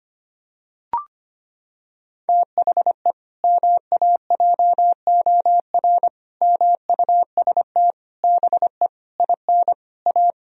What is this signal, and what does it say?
Morse code, heard as a keyed tone, about 25 wpm, 700 Hz. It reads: THE MAJOR MUST BE IN A
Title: Sentences from Top 1000 Words - Review Part 4 - 25wpm